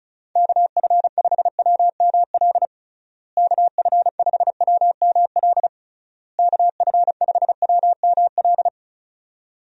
Morse code, heard as a keyed tone, about 36 wpm, 700 Hz. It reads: KF5WML KF5WML KF5WML
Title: USA Call Signs - 6 Characters - Repeated 3x - 35wpm